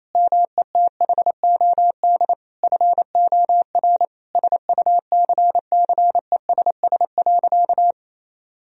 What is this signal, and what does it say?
Morse code, heard as a keyed tone, about 27 wpm, 700 Hz. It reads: METHOD FOR SUCCESS.
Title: Sentences from Top 1000 Words - Review Part 3 - 28wpm